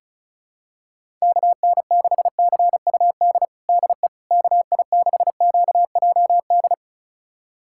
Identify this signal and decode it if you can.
Morse code, heard as a keyed tone, about 34 wpm, 700 Hz. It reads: KN6CUD DE KI6QJD
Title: USA Callsign DE Callsign - 6 Characters - 35wpm